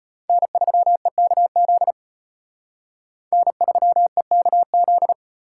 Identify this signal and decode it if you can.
Morse code, heard as a keyed tone, about 32 wpm, 700 Hz. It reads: N3EKZ N3EKZ